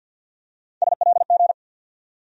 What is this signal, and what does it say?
Morse code, heard as a keyed tone, about 48 wpm, 700 Hz. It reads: SLG
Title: USA Call Sign Postfixes - 3 Letters 50wpm